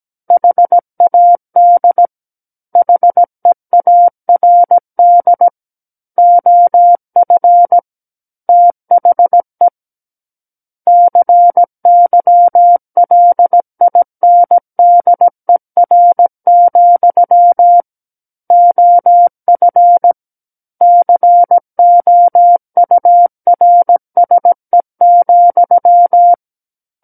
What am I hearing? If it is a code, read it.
Morse code, heard as a keyed tone, about 17 wpm, 700 Hz. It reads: HAD HEARD OF THE CYLINDER, OF COURSE,
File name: Book One / 08-Chapter 8 - 17 WPM